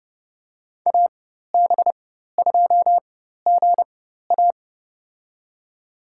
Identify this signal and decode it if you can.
Morse code, heard as a keyed tone, about 30 wpm, spaced as for 14 wpm, 700 Hz. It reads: AB2GA
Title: USA Call Signs - 5 Characters - 4x Character Spacing - 30wpm